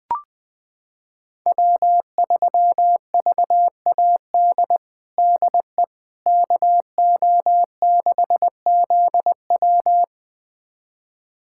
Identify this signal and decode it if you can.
Morse code, heard as a keyed tone, about 20 wpm, 700 Hz. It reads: W3VAD DE KO6ZW
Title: USA Callsign DE Callsign - 5 Characters - 20wpm